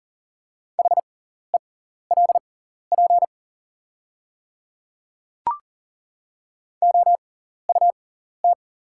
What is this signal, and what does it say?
Morse code, heard as a keyed tone, about 40 wpm, spaced as for 14 wpm, 690 Hz. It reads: HELP OUT
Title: 6x Character Spacing - Top 200 Words - 40wpm